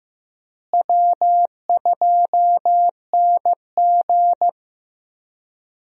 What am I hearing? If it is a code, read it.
Morse code, heard as a keyed tone, about 15 wpm, 700 Hz. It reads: W2NG
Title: USA Call Signs - 4 Characters - Repeated 3x - 15wpm